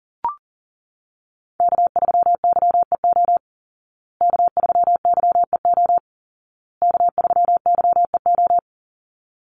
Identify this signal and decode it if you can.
Morse code, heard as a keyed tone, about 40 wpm, 700 Hz. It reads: K3YEO K3YEO K3YEO
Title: Call Signs Repeated 2x - 40wpm